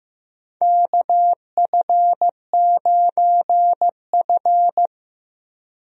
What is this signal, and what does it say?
Morse code, heard as a keyed tone, about 15 wpm, 700 Hz. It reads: KF9F
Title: USA Call Signs - 4 Characters - Repeated 2x - 15wpm